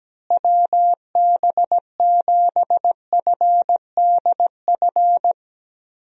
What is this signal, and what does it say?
Morse code, heard as a keyed tone, about 17 wpm, 700 Hz. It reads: WB7FDF
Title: USA Call Signs - 6 Characters - Repeated 2x - 17wpm